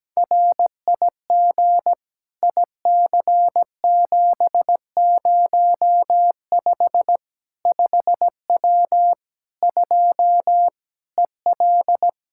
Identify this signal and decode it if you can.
Morse code, heard as a keyed tone, about 17 wpm, 700 Hz. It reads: RIG IC705 5W 2 EL
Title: K1USN SST QSO Practice 17wpm